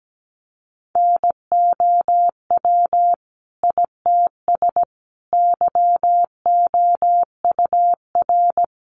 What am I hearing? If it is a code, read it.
Morse code, heard as a keyed tone, about 17 wpm, 700 Hz. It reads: NOW ITS YOUR
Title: Sentences from Top 600 Words 17wpm